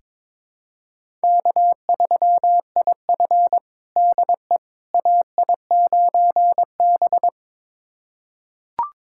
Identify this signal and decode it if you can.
Morse code, heard as a keyed tone, about 22 wpm, 700 Hz. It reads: K3IF DE AI9B
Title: USA Callsign DE Callsign - 4 Characters - 22wpm